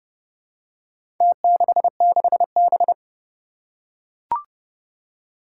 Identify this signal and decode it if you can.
Morse code, heard as a keyed tone, about 32 wpm, 700 Hz. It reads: T66B
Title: Four Challenging Letter-Number Combinations 30wpm